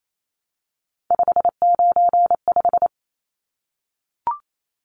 Morse code, heard as a keyed tone, about 28 wpm, 700 Hz. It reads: 595
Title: Numbers - 3-Digit 28wpm